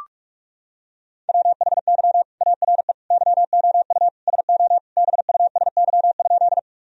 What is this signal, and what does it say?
Morse code, heard as a keyed tone, about 41 wpm, 700 Hz. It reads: WHY ARE YOU SO BUSY?
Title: Sentences from Top 1000 Words - Review Part 3 - 45wpm